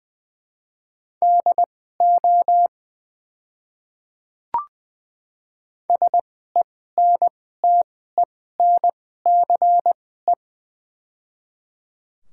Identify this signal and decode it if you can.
Morse code, heard as a keyed tone, about 20 wpm, spaced as for 14 wpm, 700 Hz. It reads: DO SENTENCE
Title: 2x Character Spacing - Top 200 Words - 20wpm